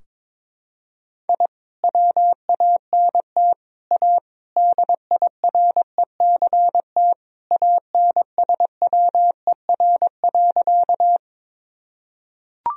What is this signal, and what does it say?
Morse code, heard as a keyed tone, about 22 wpm, 700 Hz. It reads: I WANT A DIRECT ANSWER.
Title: Sentences from Top 1000 Words - Review Part 2 - 22wpm